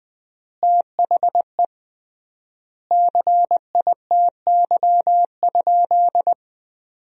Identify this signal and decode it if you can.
Morse code, heard as a keyed tone, about 20 wpm, 700 Hz. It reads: THE CITY?
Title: 3x Word-Spacing - Sentences from Top 300 Words - 20wpm